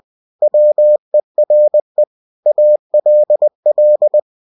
Morse code, heard as a keyed tone, about 20 wpm, 600 Hz. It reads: WERE ALL
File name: cw-books / winnie-the-pooh / winnie-20wpm-page-0079